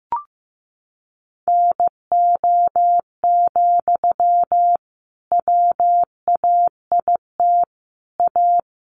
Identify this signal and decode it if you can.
Morse code, heard as a keyed tone, about 15 wpm, 700 Hz. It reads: NO, WAIT A
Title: Sentences from Top 900 Words 15wpm